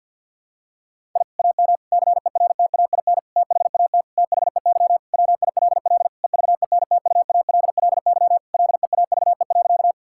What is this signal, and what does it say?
Morse code, heard as a keyed tone, about 52 wpm, 700 Hz. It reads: I AM CERTAIN THAT THEY WILL EVENTUALLY LEAVE.
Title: Sentences from Top 1000 Words - Review Part 6 - 50wpm